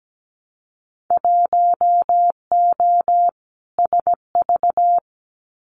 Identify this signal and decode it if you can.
Morse code, heard as a keyed tone, about 17 wpm, 700 Hz. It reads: 1O SV
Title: Winter Field Day Exchanges 17wpm